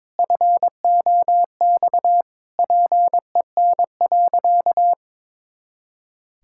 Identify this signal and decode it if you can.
Morse code, heard as a keyed tone, about 22 wpm, 700 Hz. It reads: FOX PEN.
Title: Pangram Challenge v2 - 22wpm